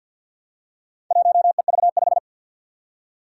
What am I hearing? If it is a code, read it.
Morse code, heard as a keyed tone, about 48 wpm, 700 Hz. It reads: 1EV5
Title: Four Challenging Letter-Number Combinations 50wpm